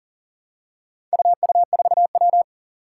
Morse code, heard as a keyed tone, about 38 wpm, 700 Hz. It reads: UU4W
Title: Four Challenging Letter-Number Combinations 40wpm